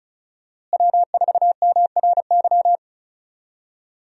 Morse code, heard as a keyed tone, about 34 wpm, 700 Hz. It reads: W4MRY
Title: USA Call Signs - 5 Characters - Repeated 2x - 35wpm